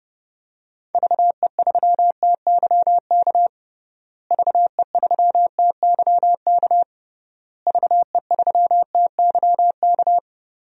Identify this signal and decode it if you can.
Morse code, heard as a keyed tone, about 30 wpm, 700 Hz. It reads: VE3TYK VE3TYK VE3TYK
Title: DXCC-100 Call Signs - 6 Characters - Repeated 3x - 30wpm